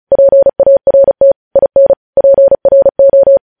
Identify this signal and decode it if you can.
Morse code, heard as a keyed tone, about 36 wpm, 550 Hz. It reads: PART IN PRO